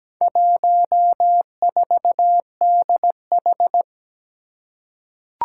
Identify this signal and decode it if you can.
Morse code, heard as a keyed tone, about 17 wpm, 700 Hz. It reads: 14DH
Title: Four Challenging Letter-Number Combinations 17wpm